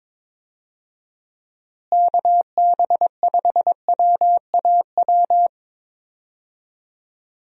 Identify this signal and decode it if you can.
Morse code, heard as a keyed tone, about 22 wpm, 700 Hz. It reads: KB5WAW
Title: USA Call Signs - 6 Characters - Repeated 3x - 22wpm